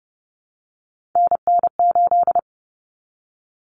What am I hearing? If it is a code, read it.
Morse code, heard as a keyed tone, about 30 wpm, 700 Hz. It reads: NN8
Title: Common Cut Numbers - 3-Digit 30wpm